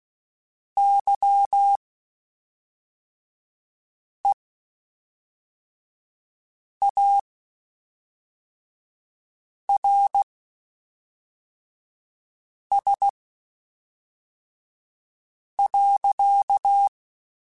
Morse code, heard as a keyed tone, about 16 wpm, spaced as for 3 wpm, 800 Hz. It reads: YEARS.